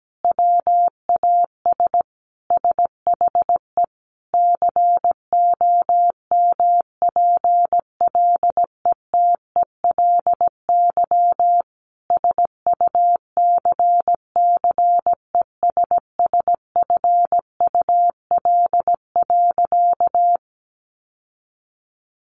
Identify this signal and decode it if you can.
Morse code, heard as a keyed tone, about 17 wpm, 700 Hz. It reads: WAS SHE COMPLETELY SUCCESSFUL.
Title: Sentences from Top 1100 Words 17wpm